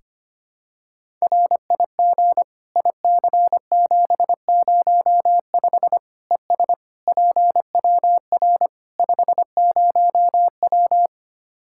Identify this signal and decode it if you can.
Morse code, heard as a keyed tone, about 25 wpm, 700 Hz. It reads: RIG IC705 ES PWR 50W
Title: Practice QSO - Exchange 2 - Info - 25wpm